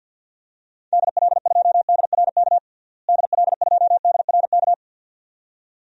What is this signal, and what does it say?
Morse code, heard as a keyed tone, about 50 wpm, 700 Hz. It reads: DL2DRK DL2DRK
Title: DXCC-100 Call Signs - 6 Characters - Repeated 2x - 50wpm